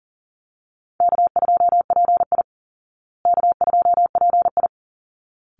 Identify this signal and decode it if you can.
Morse code, heard as a keyed tone, about 40 wpm, 700 Hz. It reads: K2PI K2PI
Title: Easy Call Signs Repeated 2x - 40wpm